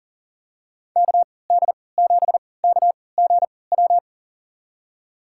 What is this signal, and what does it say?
Morse code, heard as a keyed tone, about 40 wpm, spaced as for 23 wpm, 700 Hz. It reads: KD7KGW
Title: USA Call Signs - 6 Characters - 3x Character Spacing - 40wpm